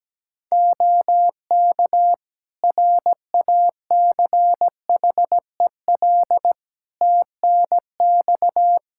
Morse code, heard as a keyed tone, about 17 wpm, 700 Hz. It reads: OK RACHEL TNX
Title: K1USN SST QSO Practice 17wpm